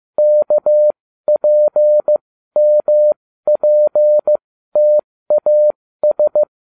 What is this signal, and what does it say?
Morse code, heard as a keyed tone, about 15 wpm, spaced as for 13 wpm, 600 Hz. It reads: KPMPTAS
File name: cw / ubacw08 2